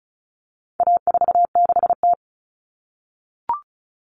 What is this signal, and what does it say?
Morse code, heard as a keyed tone, about 34 wpm, 700 Hz. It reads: A46T
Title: Common Cut Numbers - 4-Digit 35wpm